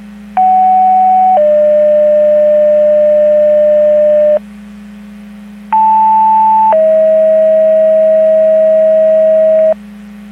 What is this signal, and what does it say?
Two-tone page, heard as a one-second tone, then a three-second tone, about 750 then 600 Hz, about 880 then 640 Hz. It is heard from 0.4 to 9.7 s.
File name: Portage 154.385 Paging